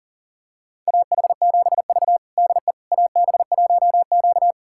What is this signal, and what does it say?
Morse code, heard as a keyed tone, about 40 wpm, 700 Hz. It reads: AH7V DE AB1Q
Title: USA Callsign DE Callsign - 4 Characters - 40wpm